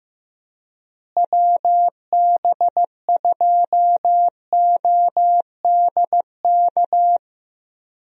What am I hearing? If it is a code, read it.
Morse code, heard as a keyed tone, about 15 wpm, 700 Hz. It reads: WB2ODK
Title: USA Call Signs - 6 Characters - Repeated 2x - 15wpm